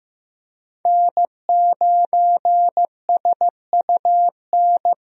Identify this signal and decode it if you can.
Morse code, heard as a keyed tone, about 15 wpm, 700 Hz. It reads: N9SUN
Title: USA Call Signs - 5 Characters - Repeated 3x - 15wpm